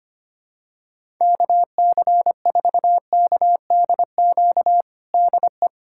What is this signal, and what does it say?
Morse code, heard as a keyed tone, about 25 wpm, 700 Hz. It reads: KC4KDQ DE
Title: USA Callsign DE Callsign - 6 Characters - 25wpm